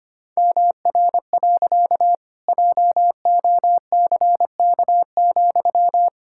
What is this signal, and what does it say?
Morse code, heard as a keyed tone, about 25 wpm, 700 Hz. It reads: MR. JOCK,
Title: Pangram Challenge v2 - 25wpm